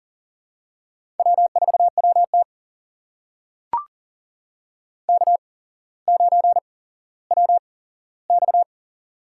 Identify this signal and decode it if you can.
Morse code, heard as a keyed tone, about 40 wpm, spaced as for 17 wpm, 700 Hz. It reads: W4WT K9WX
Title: USA Call Signs - 4 Characters - 8x Character Spacing - 40wpm